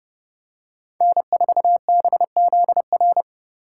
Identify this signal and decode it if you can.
Morse code, heard as a keyed tone, about 30 wpm, 700 Hz. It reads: N 4 B Z R
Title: USA Call Signs - 5 Characters - 6x Character Spacing - 30wpm